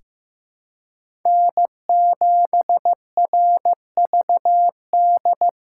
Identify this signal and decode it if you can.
Morse code, heard as a keyed tone, about 15 wpm, 700 Hz. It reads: N7RVD
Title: K1USN SST QSO Practice - 15wpm